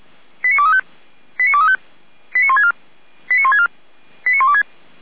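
Five-tone selective call (ZVEI), heard as quick runs of five tones, tones of about 70 ms. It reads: 89235 89235 89164 89164 89126
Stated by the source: ZVEI five-tone selective call received by SDR.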